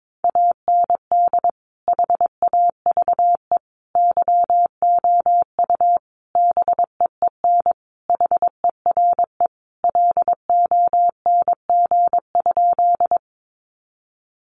Morse code, heard as a keyed tone, about 22 wpm, 700 Hz. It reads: AND HAVE YOU BEEN HERE LONG?
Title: Sentences from Top 200 Words 22wpm